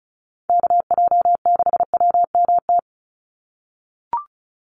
Morse code, heard as a keyed tone, about 34 wpm, 700 Hz. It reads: KJ6WMT
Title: USA Call Signs 35wpm